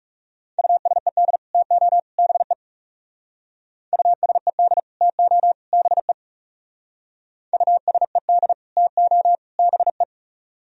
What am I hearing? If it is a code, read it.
Morse code, heard as a keyed tone, about 35 wpm, 700 Hz. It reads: USED TO BE USED TO BE USED TO BE
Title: Speed-Racing - Sets of 3 Words 30wpm